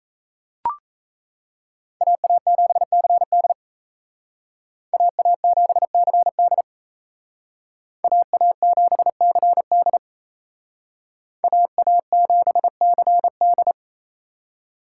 Morse code, heard as a keyed tone, about 36 wpm, 700 Hz. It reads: AA7CD AA7CD AA7CD AA7CD